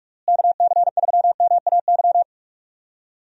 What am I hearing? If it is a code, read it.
Morse code, heard as a keyed tone, about 46 wpm, 700 Hz. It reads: KC3MAY
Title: USA Call Signs - 6 Characters - Repeated 3x - 45wpm